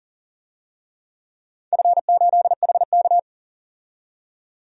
Morse code, heard as a keyed tone, about 40 wpm, 700 Hz. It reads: F8HK
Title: DXCC-100 Call Signs - 4 Characters - 40wpm